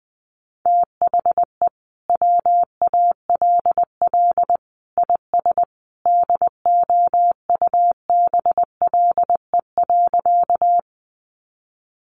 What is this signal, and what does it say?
Morse code, heard as a keyed tone, about 19 wpm, 700 Hz. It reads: THE WALL IS DOUBLE.
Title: Sentences from Top 1400 Words 20wpm